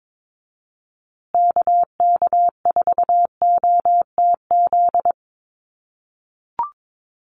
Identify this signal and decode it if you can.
Morse code, heard as a keyed tone, about 22 wpm, 700 Hz. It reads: KK4OTZ
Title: USA Call Signs Repeated 3x 22wpm